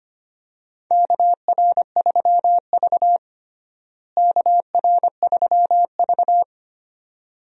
Morse code, heard as a keyed tone, about 25 wpm, 700 Hz. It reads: KR3V KR3V
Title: USA Call Signs - 4 Characters - Repeated 2x - 25wpm